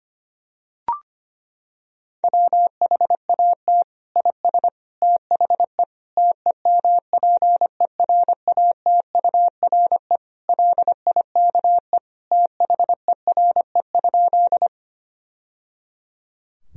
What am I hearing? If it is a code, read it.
Morse code, heard as a keyed tone, about 25 wpm, 700 Hz. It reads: WHAT IS THE TEMPERATURE LIKE THERE?
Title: Sentences from Top 1000 Words - Review Part 4 - 25wpm